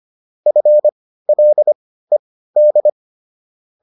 Morse code, heard as a keyed tone, about 25 wpm, spaced as for 15 wpm, 600 Hz. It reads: FLED